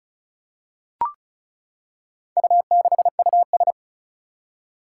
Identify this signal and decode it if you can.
Morse code, heard as a keyed tone, about 36 wpm, 700 Hz. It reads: U6US
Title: Four Challenging Letter-Number Combinations 35wpm